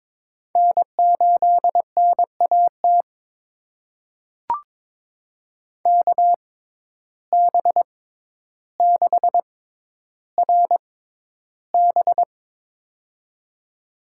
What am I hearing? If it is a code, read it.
Morse code, heard as a keyed tone, about 23 wpm, spaced as for 11 wpm, 700 Hz. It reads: N8NAT KB6RB
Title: USA Call Signs - 5 Characters - 6x Character Spacing - 22wpm